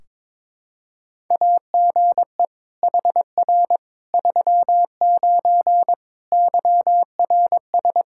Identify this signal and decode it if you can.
Morse code, heard as a keyed tone, about 22 wpm, 700 Hz. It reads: AGE HR 39 YRS
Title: Practice QSO - Exchange 3 - More Info - 22wpm